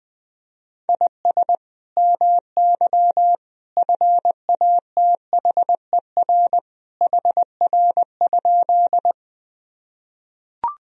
Morse code, heard as a keyed tone, about 20 wpm, 700 Hz. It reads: IS MY FATHER HR?
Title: Sentences from Top 300 Words and Abbreviations 20wpm